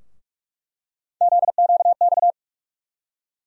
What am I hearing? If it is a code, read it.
Morse code, heard as a keyed tone, about 46 wpm, 700 Hz. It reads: ZQX
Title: USA Call Sign Postfixes - 3 Letters 45wpm